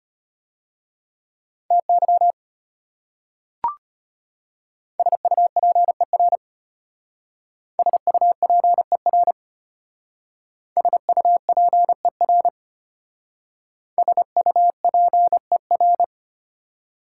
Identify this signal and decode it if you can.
Morse code, heard as a keyed tone, about 34 wpm, 700 Hz. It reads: TY SUPER SUPER SUPER SUPER